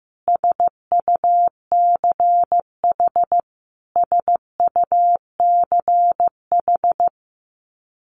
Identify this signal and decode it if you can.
Morse code, heard as a keyed tone, about 15 wpm, 700 Hz. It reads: SUCH SUCH
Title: Top 300 Words Repeated 15wpm